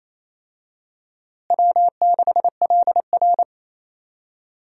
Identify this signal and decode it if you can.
Morse code, heard as a keyed tone, about 28 wpm, 700 Hz. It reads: W6LR
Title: USA Call Signs - 4 Characters - Repeated 3x - 28wpm